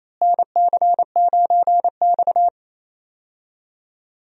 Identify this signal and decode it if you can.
Morse code, heard as a keyed tone, about 28 wpm, 700 Hz. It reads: NC9X
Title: USA Call Signs - 4 Characters - Repeated 3x - 28wpm